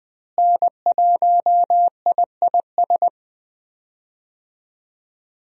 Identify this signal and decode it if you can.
Morse code, heard as a keyed tone, about 20 wpm, 700 Hz. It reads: N1IIS
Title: USA Call Signs - 5 Characters - Repeated 3x - 20wpm